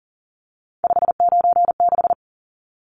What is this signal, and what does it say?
Morse code, heard as a keyed tone, about 40 wpm, 700 Hz. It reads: 596
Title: Numbers - 3-Digit 40wpm